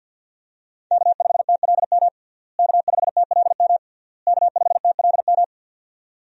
Morse code, heard as a keyed tone, about 51 wpm, 700 Hz. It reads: K5TLM K5TLM K5TLM
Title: USA Call Signs - 5 Characters - Repeated 3x - 50wpm